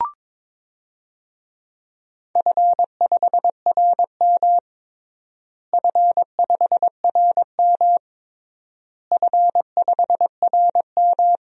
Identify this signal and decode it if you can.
Morse code, heard as a keyed tone, about 22 wpm, 700 Hz. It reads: F5RM F5RM F5RM
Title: DXCC-100 Call Signs - 4 Characters - Repeated 3x - 22wpm